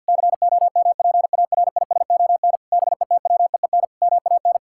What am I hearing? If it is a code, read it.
Morse code, heard as a keyed tone, about 50 wpm, 700 Hz. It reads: COMPARISON BETWEEN MAN
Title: Sentences from Top 1600 Words 50wpm